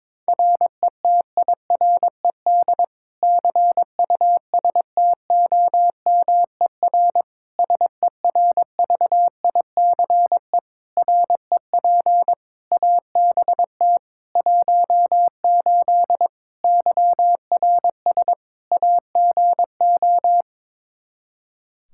Morse code, heard as a keyed tone, about 22 wpm, 700 Hz. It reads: RETIRED CUSTOMER SERVICE REP ABT 18 YRS AGO
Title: Practice QSO - Exchange 3 - More Info - 22wpm